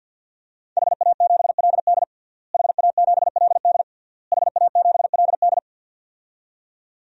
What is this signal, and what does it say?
Morse code, heard as a keyed tone, about 46 wpm, 700 Hz. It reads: HA7LD HA7LD HA7LD
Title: DXCC-100 Call Signs - 5 Characters - Repeated 3x - 50wpm